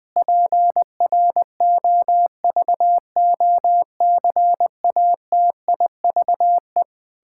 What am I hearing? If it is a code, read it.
Morse code, heard as a keyed tone, about 20 wpm, 700 Hz. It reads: PROVOCATIVE
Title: Pangram Challenge v2 - 20wpm